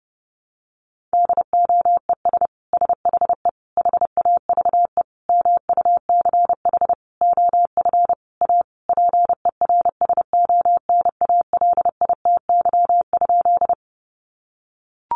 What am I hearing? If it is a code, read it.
Morse code, heard as a keyed tone, about 30 wpm, 700 Hz. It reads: DOES SHE HAVE MUCH OF A PERSONALITY?
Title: Sentences from Top 1400 Words 30wpm